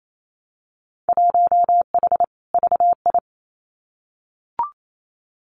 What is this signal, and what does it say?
Morse code, heard as a keyed tone, about 28 wpm, 700 Hz. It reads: 1H VI
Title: Winter Field Day Exchanges Encore 28wpm